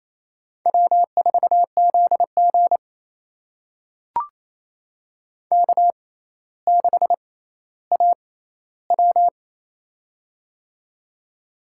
Morse code, heard as a keyed tone, about 27 wpm, spaced as for 14 wpm, 700 Hz. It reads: W4ZG K6AW
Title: USA Call Signs - 4 Characters - 6x Character Spacing - 28wpm